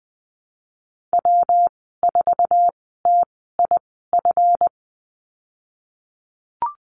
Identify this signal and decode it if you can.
Morse code, heard as a keyed tone, about 20 wpm, spaced as for 15 wpm, 700 Hz. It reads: W4TIF
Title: 179 Lesson 29g - Slash - Call Signs 20wpm